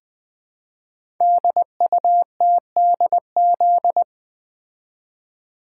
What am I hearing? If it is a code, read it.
Morse code, heard as a keyed tone, about 20 wpm, 700 Hz. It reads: DUTDZ
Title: Code Groups - 5 Letters - 20wpm